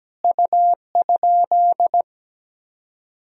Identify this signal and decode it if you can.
Morse code, heard as a keyed tone, about 17 wpm, 700 Hz. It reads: U?
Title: Sentences from Top 100 Words and Abbreviations 17wpm